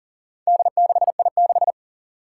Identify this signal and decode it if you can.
Morse code, heard as a keyed tone, about 37 wpm, 700 Hz. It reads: D6I6
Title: Four Challenging Letter-Number Combinations 40wpm